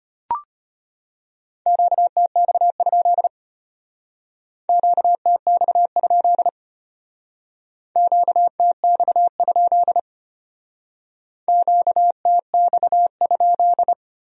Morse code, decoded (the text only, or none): QTX? QTX? QTX? QTX?